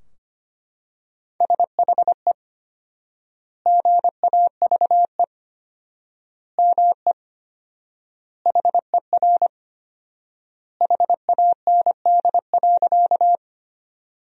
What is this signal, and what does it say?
Morse code, heard as a keyed tone, about 25 wpm, 700 Hz. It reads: SHE GAVE ME HER HAND.
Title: 4x Word-Spacing - Sentences from Top 500 Words - 25wpm